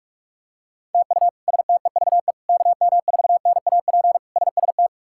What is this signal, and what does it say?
Morse code, heard as a keyed tone, about 46 wpm, 700 Hz. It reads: TU STEVE KM4NAP SST
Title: K1USN SST QSO Practice - 45wpm